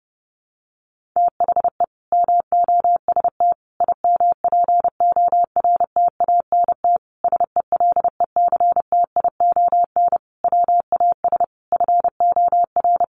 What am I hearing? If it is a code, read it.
Morse code, heard as a keyed tone, about 30 wpm, 700 Hz. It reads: THE MOST IMPORTANT SELECTION WAS FOR
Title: Sentences from Top 1400 Words 30wpm